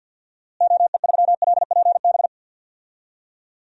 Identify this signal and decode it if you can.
Morse code, heard as a keyed tone, about 53 wpm, 700 Hz. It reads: OE3LPB
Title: DXCC-100 Call Signs - 6 Characters - Repeated 2x - 50wpm